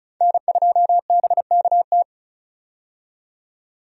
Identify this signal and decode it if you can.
Morse code, heard as a keyed tone, about 36 wpm, 700 Hz. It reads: N2BKT
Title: USA Call Signs - 5 Characters - 35wpm